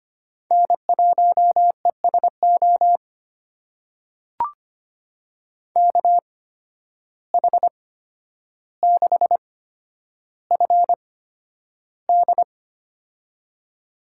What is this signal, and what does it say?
Morse code, heard as a keyed tone, about 25 wpm, spaced as for 11 wpm, 700 Hz. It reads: N1ESO KH6FD
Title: USA Call Signs - 5 Characters - 8x Character Spacing - 25wpm